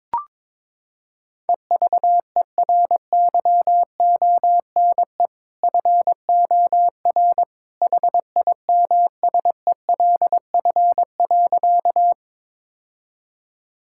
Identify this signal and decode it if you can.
Morse code, heard as a keyed tone, about 23 wpm, 700 Hz. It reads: EVERYONE FOR HIMSELF.
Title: Sentences from Top 1600 Words 22wpm